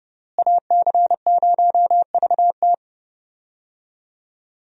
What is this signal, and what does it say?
Morse code, heard as a keyed tone, about 31 wpm, 700 Hz. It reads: AC0VT
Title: USA Call Signs - 5 Characters - Repeated 2x - 30wpm